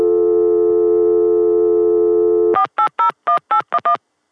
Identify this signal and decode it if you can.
DTMF, heard as telephone touch tones, tones of about 100 ms. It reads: 4801811